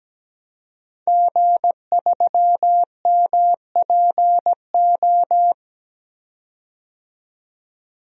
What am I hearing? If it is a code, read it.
Morse code, heard as a keyed tone, about 17 wpm, 700 Hz. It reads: G3MPO
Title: DXCC-100 Call Signs - 5 Characters - 17wpm